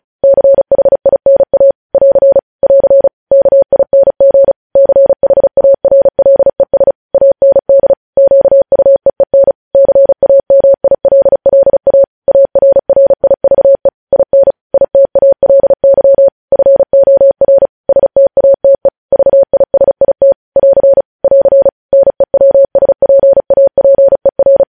Morse code, heard as a keyed tone, about 36 wpm, 550 Hz. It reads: CHINA <AR> <AR> KING CHARLES AND QUEEN CAMILLA ARRIVE IN ITALY FOR STATE VISIT <AR> <AR> NEWSPAPER
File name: podcast / News headlines-1x-35x35